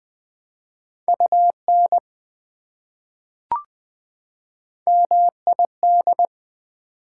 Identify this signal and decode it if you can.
Morse code, heard as a keyed tone, about 20 wpm, 700 Hz. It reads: UN MID
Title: Prefixes and Suffixes 20wpm